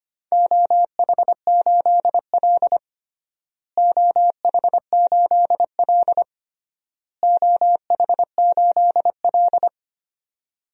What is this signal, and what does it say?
Morse code, heard as a keyed tone, about 25 wpm, 700 Hz. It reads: OH8L OH8L OH8L
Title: DXCC-100 Call Signs - 4 Characters - Repeated 3x - 25wpm